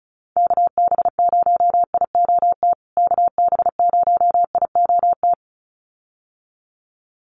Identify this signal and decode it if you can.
Morse code, heard as a keyed tone, about 34 wpm, 700 Hz. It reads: KB0IOT KB0IOT
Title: USA Call Signs Repeated 2x 35wpm